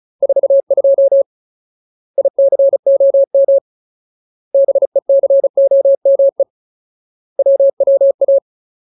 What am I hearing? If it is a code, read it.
Morse code, heard as a keyed tone, about 34 wpm, 550 Hz. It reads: 42 ICOM BECOME WWA
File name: Ultimate Word List / 35WPM Ultimate